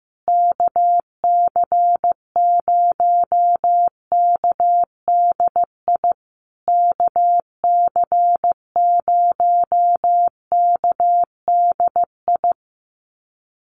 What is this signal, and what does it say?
Morse code, heard as a keyed tone, about 15 wpm, 700 Hz. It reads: KC0KDI KC0KDI
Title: USA Call Signs Repeated 3x 15wpm